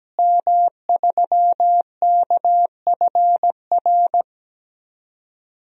Morse code, heard as a keyed tone, about 17 wpm, 700 Hz. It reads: M3KFR
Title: DXCC-100 Call Signs - 5 Characters - Repeated 2x - 17wpm